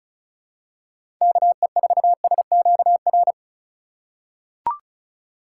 Morse code, heard as a keyed tone, about 34 wpm, 700 Hz. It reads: KE4SQR
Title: USA Call Signs - 6 Characters - 2x Character Spacing - 35wpm